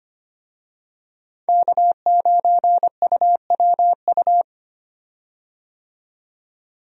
Morse code, heard as a keyed tone, about 25 wpm, 700 Hz. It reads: K9UWU
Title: USA Call Signs - 5 Characters - 25wpm